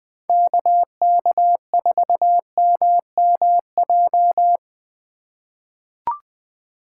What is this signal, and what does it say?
Morse code, heard as a keyed tone, about 20 wpm, 700 Hz. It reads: KK4MMJ
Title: USA Call Signs - 4x Character Spacing - 20wpm